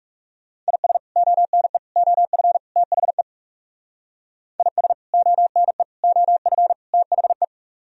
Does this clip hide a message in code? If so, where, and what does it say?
Yes: Morse code from 0.7 to 7.4 s, heard as a keyed tone, about 38 wpm, 700 Hz. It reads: IS ONE OF THE IS ONE OF THE